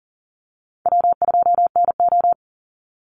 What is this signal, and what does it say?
Morse code, heard as a keyed tone, about 40 wpm, 700 Hz. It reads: W2NO
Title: Easy Call Signs Repeated 3x - 40wpm